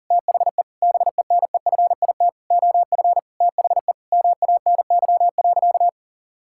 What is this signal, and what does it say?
Morse code, heard as a keyed tone, about 40 wpm, 700 Hz. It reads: THE BENEFIT OF THE MANY.
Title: Sentences from Top 1000 Words - Review Part 6 - 40wpm